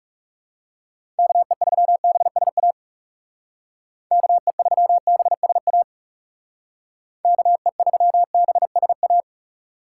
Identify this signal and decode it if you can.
Morse code, heard as a keyed tone, about 36 wpm, 700 Hz. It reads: KE3BSA KE3BSA KE3BSA
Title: Speed Racing - USA 2x3 Call Signs 30wpm